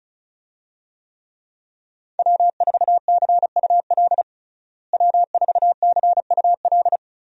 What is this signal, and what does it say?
Morse code, heard as a keyed tone, about 34 wpm, 700 Hz. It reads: W4CUL W4CUL
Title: USA Call Signs - 5 Characters - Repeated 2x - 35wpm